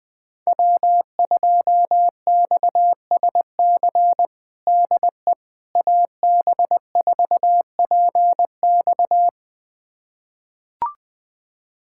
Morse code, heard as a keyed tone, about 20 wpm, 700 Hz. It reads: W2XSC DE AB4PX
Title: USA Callsign DE Callsign - 5 Characters - 20wpm